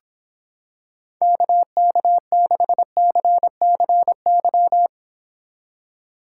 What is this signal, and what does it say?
Morse code, heard as a keyed tone, about 26 wpm, 700 Hz. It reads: KK6CCY